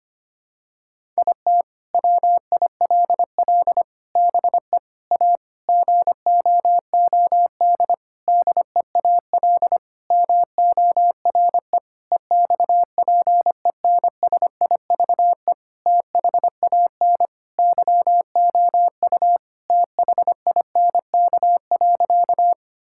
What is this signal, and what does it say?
Morse code, heard as a keyed tone, about 25 wpm, 700 Hz. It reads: IT WILL BE A GOOD DEAL MORE EXPENSIVE THAN YOU THINK.
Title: Sentences from Top 1000 Words - Review Part 6 - 25wpm